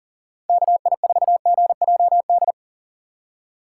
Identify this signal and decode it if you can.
Morse code, heard as a keyed tone, about 40 wpm, 700 Hz. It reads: KI4GJD
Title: USA Call Signs - 6 Characters - 4x Character Spacing - 40wpm